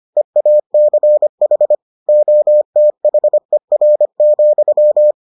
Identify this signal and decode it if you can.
Morse code, heard as a keyed tone, about 25 wpm, 600 Hz. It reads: EACH OTHER,